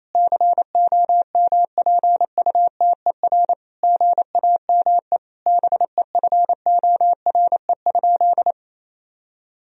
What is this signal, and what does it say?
Morse code, heard as a keyed tone, about 28 wpm, 700 Hz. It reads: COMPUTER GAME BEFORE?
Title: Sentences from Top 1000 Words - Review Part 3 - 28wpm